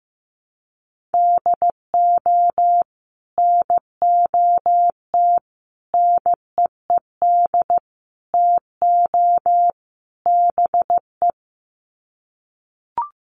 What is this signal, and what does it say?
Morse code, heard as a keyed tone, about 15 wpm, 700 Hz. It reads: DO NOT NEED TO BE
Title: Sets of 5 Words Encore 15wpm